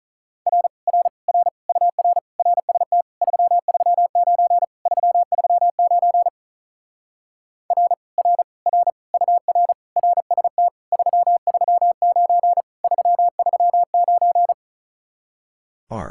Morse code, heard as a keyed tone, about 44 wpm, 700 Hz. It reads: R R R UR RST 339 339 R R R UR RST 339 339